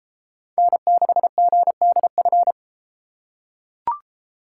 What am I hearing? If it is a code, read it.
Morse code, heard as a keyed tone, about 29 wpm, 700 Hz. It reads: N6GDF